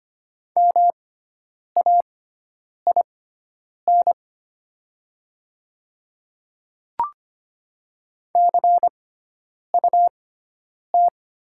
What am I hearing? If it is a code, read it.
Morse code, heard as a keyed tone, about 25 wpm, spaced as for 9 wpm, 700 Hz. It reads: MAIN CUT
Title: 6x Character Spacing - Top 300 Words - 25wpm